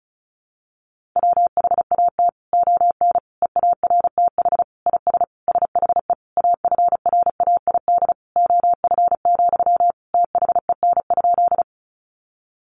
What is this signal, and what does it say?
Morse code, heard as a keyed tone, about 34 wpm, 700 Hz. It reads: WHAT ON EARTH IS SHE AFRAID OF, THEN?
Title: Sentences from Top 1500 Words 35wpm